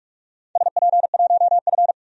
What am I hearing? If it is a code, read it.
Morse code, heard as a keyed tone, about 46 wpm, 700 Hz. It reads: SP1F
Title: DXCC-100 Call Signs - 4 Characters - 45wpm